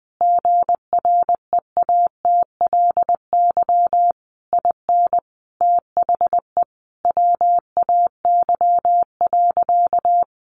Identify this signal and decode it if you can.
Morse code, heard as a keyed tone, about 20 wpm, 700 Hz. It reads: GREATLY IN THE WAY.
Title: Sentences from Top 1400 Words 20wpm